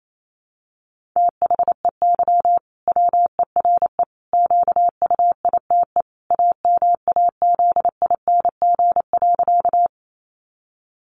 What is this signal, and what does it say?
Morse code, heard as a keyed tone, about 28 wpm, 700 Hz. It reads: THEY WERE QUITE AMAZING.
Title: Sentences from Top 1500 Words 28wpm